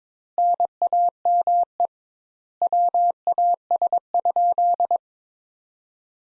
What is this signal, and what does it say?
Morse code, heard as a keyed tone, about 22 wpm, 700 Hz. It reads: NAME WAS?
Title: Sentences Top 200 Words - 22wpm